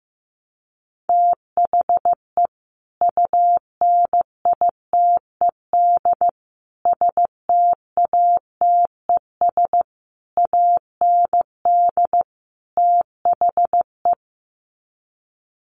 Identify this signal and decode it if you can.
Morse code, heard as a keyed tone, about 15 wpm, 700 Hz. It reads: THE UNITED STATES AND THE
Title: Sets of 5 Words 15wpm